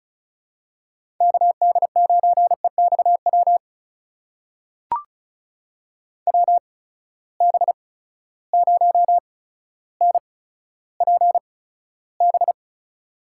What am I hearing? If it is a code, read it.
Morse code, heard as a keyed tone, about 36 wpm, spaced as for 15 wpm, 700 Hz. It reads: KD9EXW WB0NPB
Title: USA Call Signs - 6 Characters - 8x Character Spacing - 35wpm